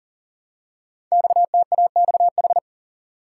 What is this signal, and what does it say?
Morse code, heard as a keyed tone, about 40 wpm, 690 Hz. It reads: XTAXH
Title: Code Groups - 5 Letters - 40wpm